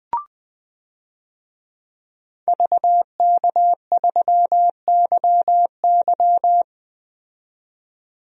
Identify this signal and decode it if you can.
Morse code, heard as a keyed tone, about 20 wpm, 700 Hz. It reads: VK3YY
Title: DXCC-100 Call Signs - 5 Characters - 20wpm